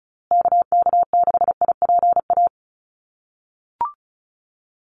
Morse code, heard as a keyed tone, about 34 wpm, 700 Hz. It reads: KK6IPA
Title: USA Call Signs Repeated 2x 35wpm